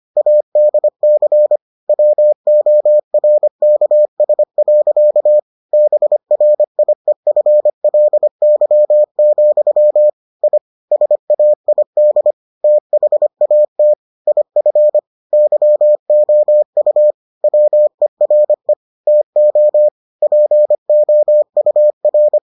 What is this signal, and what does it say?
Morse code, heard as a keyed tone, about 25 wpm, 600 Hz. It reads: ADC WORKS. BRIEFLY, I SAID THAT IF YOU WERE TO POUR